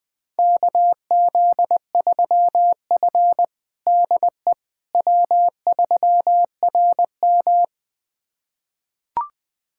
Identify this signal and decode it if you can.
Morse code, heard as a keyed tone, about 21 wpm, 700 Hz. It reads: KZ3F DE W3RM
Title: USA Callsign DE Callsign - 4 Characters - 20wpm